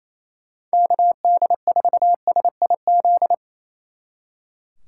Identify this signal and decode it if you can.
Morse code, heard as a keyed tone, about 35 wpm, 700 Hz. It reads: KD4SIZ